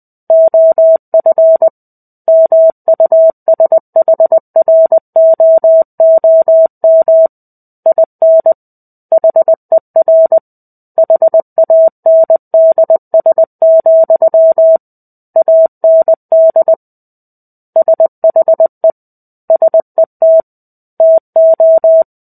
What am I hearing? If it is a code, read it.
Morse code, heard as a keyed tone, about 20 wpm, 650 Hz. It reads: OF MUSHROOM IN HER HANDS, AND SHE SET TO
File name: 05-Chapter-05 20 WPM